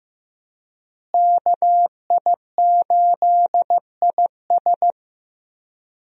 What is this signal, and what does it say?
Morse code, heard as a keyed tone, about 15 wpm, 700 Hz. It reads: KI8IS
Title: USA Call Signs - 5 Characters - Repeated 3x - 15wpm